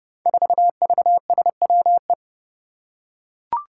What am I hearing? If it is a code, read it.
Morse code, heard as a keyed tone, about 30 wpm, 700 Hz. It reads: 4VSWE
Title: Five Challenging Letter-Number Combinations 30wpm